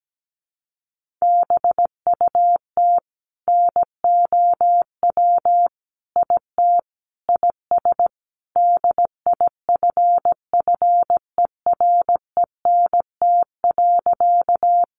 Morse code, heard as a keyed tone, about 17 wpm, 700 Hz. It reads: BUT NOW IT IS DIFFERENT.
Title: Sentences from Top 600 Words 17wpm